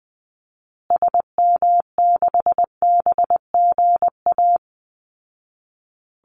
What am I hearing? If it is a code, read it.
Morse code, heard as a keyed tone, about 20 wpm, 700 Hz. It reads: SM6BGA
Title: Call Signs Repeated 2x - 20wpm